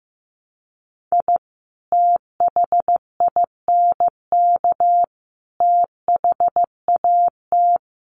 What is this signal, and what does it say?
Morse code, heard as a keyed tone, about 15 wpm, 700 Hz. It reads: I THINK THAT
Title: Sets of 5 Words Encore2 15wpm